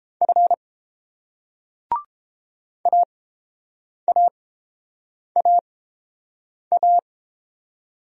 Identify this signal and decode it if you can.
Morse code, heard as a keyed tone, about 31 wpm, spaced as for 7 wpm, 700 Hz. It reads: F AAAA